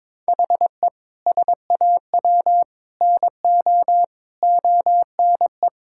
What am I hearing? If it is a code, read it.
Morse code, heard as a keyed tone, about 22 wpm, 700 Hz. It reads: HE SAW NO ONE
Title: Sentences from Top 1000 Words - Review Part 6 - 22wpm